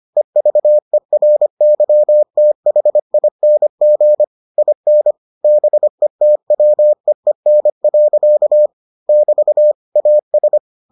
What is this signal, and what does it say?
Morse code, heard as a keyed tone, about 25 wpm, 600 Hz. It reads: EVERYTHING IN BETWEEN. <BT> AS